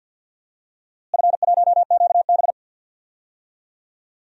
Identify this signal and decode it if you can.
Morse code, heard as a keyed tone, about 49 wpm, 700 Hz. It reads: F1QB
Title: DXCC-100 Call Signs - 4 Characters - Repeated 2x - 50wpm